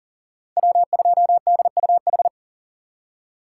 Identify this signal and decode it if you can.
Morse code, heard as a keyed tone, about 40 wpm, 700 Hz. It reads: W2DUH
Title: USA Call Signs - 2x Character Spacing - 40wpm